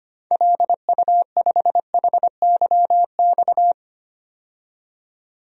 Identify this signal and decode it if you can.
Morse code, heard as a keyed tone, about 25 wpm, 700 Hz. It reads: LU5HYX
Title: DXCC-100 Call Signs - 6 Characters - 25wpm